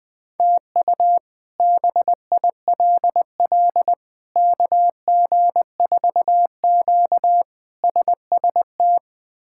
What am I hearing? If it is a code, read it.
Morse code, heard as a keyed tone, about 21 wpm, 700 Hz. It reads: TU BILL KG4Q SST
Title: K1USN SST QSO Practice - 20wpm